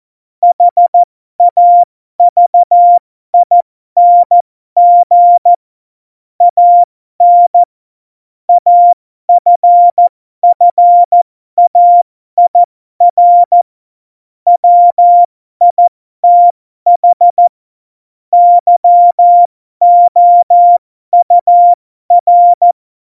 Morse code, heard as a keyed tone, about 14 wpm, spaced as for 12 wpm, 700 Hz. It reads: HAVING AN AFFAIR WITH YOUR
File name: Morsekurs / Witze